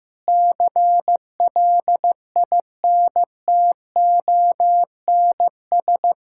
Morse code, heard as a keyed tone, about 15 wpm, 700 Hz. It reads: CLINTONS
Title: Pangram Challenge v2 - 15wpm